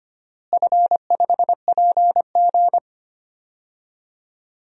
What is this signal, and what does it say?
Morse code, heard as a keyed tone, about 25 wpm, 700 Hz. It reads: F5PG
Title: DXCC-100 Call Signs - 4 Characters - Repeated 2x - 25wpm